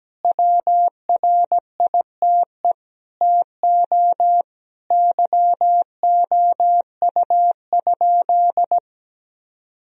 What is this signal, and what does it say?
Morse code, heard as a keyed tone, about 17 wpm, 700 Hz. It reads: WRITE TO YOU?
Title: Sentences from Top 1000 Words - Review Part 3 - 17wpm